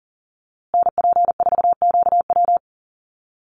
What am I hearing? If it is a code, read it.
Morse code, heard as a keyed tone, about 39 wpm, 700 Hz. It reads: NP4QW
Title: USA Call Signs Repeated 3x 40wpm